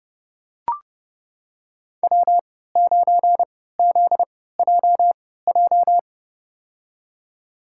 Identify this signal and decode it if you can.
Morse code, heard as a keyed tone, about 30 wpm, spaced as for 17 wpm, 700 Hz. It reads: W9ZJJ
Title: USA Call Signs - 5 Characters - 3x Character Spacing - 30wpm